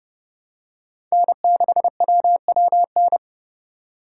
Morse code, heard as a keyed tone, about 30 wpm, 700 Hz. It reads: N6WWN
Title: USA Call Signs - 3x Character Spacing - 30wpm